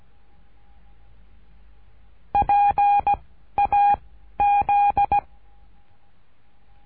Morse code, heard as a keyed tone, about 17 wpm, 810 Hz. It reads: P A Z
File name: cursos / CURSO DE TELEGRAFIA / Telegrafia - Trilha 13